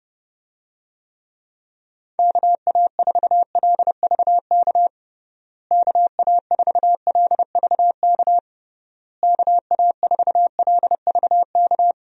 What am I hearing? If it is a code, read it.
Morse code, heard as a keyed tone, about 30 wpm, 700 Hz. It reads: KA4LVK KA4LVK KA4LVK
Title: USA Call Signs - 6 Characters - Repeated 3x - 30wpm